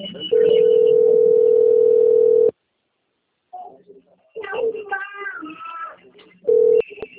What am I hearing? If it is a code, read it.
Ringback tone (North America), 440 + 480 Hz, with speech.
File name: out- - - - - .32607